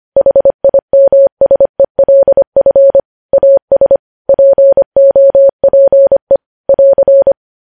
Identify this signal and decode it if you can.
Morse code, heard as a keyed tone, about 25 wpm, 550 Hz. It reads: HIMSELF AS POPE <AR>
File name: podcast / News headlines-1x-25x25